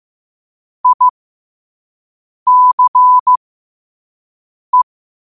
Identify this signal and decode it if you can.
Morse code, heard as a keyed tone, about 15 wpm, spaced as for 5 wpm, 1000 Hz. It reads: ICE
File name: qotdfiles / 5wpm